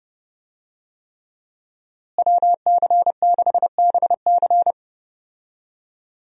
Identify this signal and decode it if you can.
Morse code, heard as a keyed tone, about 30 wpm, 700 Hz. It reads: WC6BC
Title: USA Call Signs - 5 Characters - Repeated 3x - 30wpm